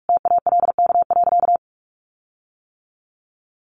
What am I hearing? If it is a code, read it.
Morse code, heard as a keyed tone, about 44 wpm, 700 Hz. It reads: TALK.
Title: Sentences from Top 1300 Words 45wpm